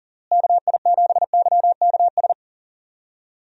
Morse code, heard as a keyed tone, about 40 wpm, 700 Hz. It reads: KI7YKS
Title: USA Call Signs - 6 Characters - Repeated 3x - 40wpm